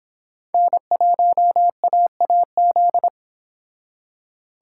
Morse code, heard as a keyed tone, about 31 wpm, 700 Hz. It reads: N1AAZ